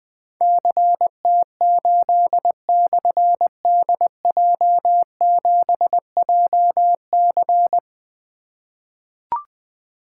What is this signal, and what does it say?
Morse code, heard as a keyed tone, about 20 wpm, 700 Hz. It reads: CT8/DJ7JC
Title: Difficult Call Signs 20wpm